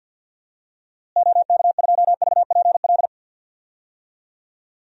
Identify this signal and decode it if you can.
Morse code, heard as a keyed tone, about 53 wpm, 700 Hz. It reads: OK2VPL
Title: DXCC-100 Call Signs - 6 Characters - Repeated 2x - 50wpm